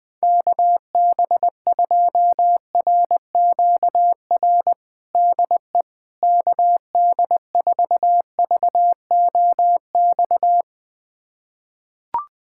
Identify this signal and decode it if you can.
Morse code, heard as a keyed tone, about 21 wpm, 700 Hz. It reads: KB2RQR DE KD4VOX
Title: USA Callsign DE Callsign - 6 Characters - 20wpm